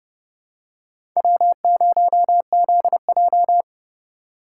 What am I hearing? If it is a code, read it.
Morse code, heard as a keyed tone, about 30 wpm, 700 Hz. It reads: W0ZJ
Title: USA Call Signs - 4x Character Spacing - 30wpm